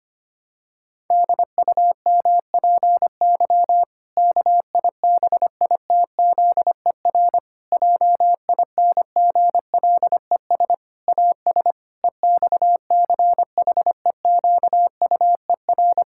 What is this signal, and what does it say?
Morse code, heard as a keyed tone, about 25 wpm, 700 Hz. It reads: DUMPY KIBITZER JINGLES AS EXCHEQUER
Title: Pangram Challenge v2 - 25wpm